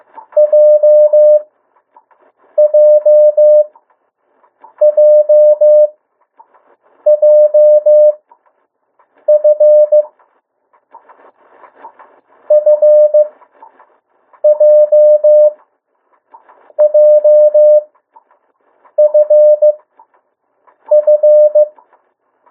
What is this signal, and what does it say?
Morse code, heard as a keyed tone, about 15 wpm, spaced as for 6 wpm, 600 Hz. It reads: JJJJF FJJFF
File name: urok 1